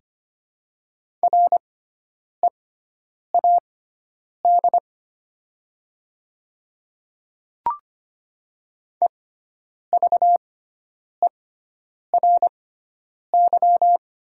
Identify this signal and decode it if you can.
Morse code, heard as a keyed tone, about 25 wpm, spaced as for 9 wpm, 700 Hz. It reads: READ EVERY
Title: 6x Character Spacing - Top 200 Words - 25wpm